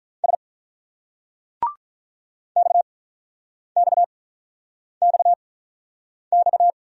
Morse code, heard as a keyed tone, about 46 wpm, spaced as for 9 wpm, 690 Hz. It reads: S XXXX